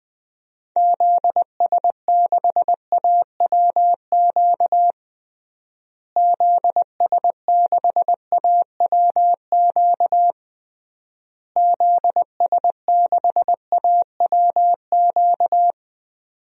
Morse code, heard as a keyed tone, about 20 wpm, 700 Hz. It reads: ZS6AWQ ZS6AWQ ZS6AWQ
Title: DXCC-100 Call Signs - 6 Characters - Repeated 3x - 20wpm